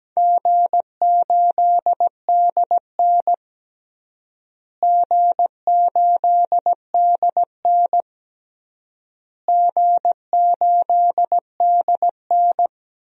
Morse code, heard as a keyed tone, about 17 wpm, 700 Hz. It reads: G8DN G8DN G8DN
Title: DXCC-100 Call Signs - 4 Characters - Repeated 3x - 17wpm